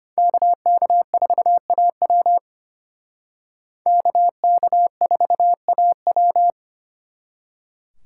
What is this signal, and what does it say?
Morse code, heard as a keyed tone, about 32 wpm, 700 Hz. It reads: KK4AW KK4AW